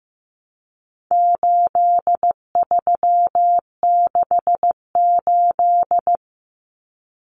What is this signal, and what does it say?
Morse code, heard as a keyed tone, about 15 wpm, 700 Hz. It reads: 8368
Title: Numbers - 4-Digit 15wpm